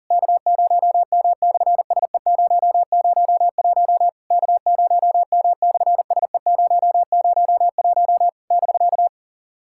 Morse code, heard as a keyed tone, about 37 wpm, 700 Hz. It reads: K0M/SE001 K0M/SE001 <BK>
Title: SOTA QSO Practice 40wpm